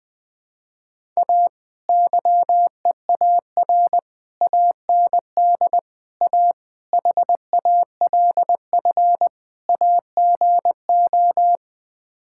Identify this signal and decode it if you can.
Morse code, heard as a keyed tone, about 20 wpm, 700 Hz. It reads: A YEAR AND A HALF AGO
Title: Sets of 6 Words - 20wpm